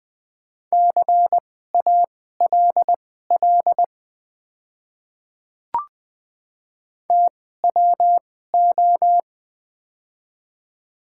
Morse code, heard as a keyed tone, about 20 wpm, spaced as for 14 wpm, 700 Hz. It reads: CALL TWO
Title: 2x Character Spacing - Top 100 Words - 20wpm